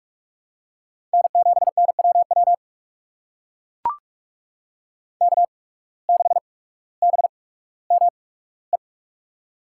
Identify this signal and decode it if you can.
Morse code, heard as a keyed tone, about 44 wpm, spaced as for 19 wpm, 700 Hz. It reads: N7NWW K6BME
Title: USA Call Signs - 5 Characters - 8x Character Spacing - 45wpm